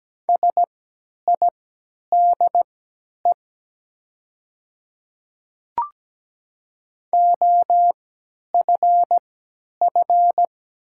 Morse code, heard as a keyed tone, about 17 wpm, spaced as for 10 wpm, 700 Hz. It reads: SIDE OFF
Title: 3x Character Spacing - Top 500 Words - 17wpm